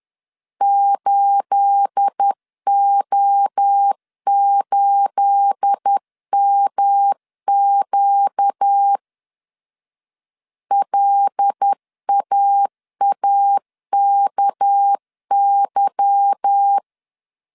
Morse code, heard as a keyed tone, about 10 wpm, 800 Hz. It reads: 8O8MQ LAAKY
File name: CW / DET / 1 9 1500C S A 10wpm 4511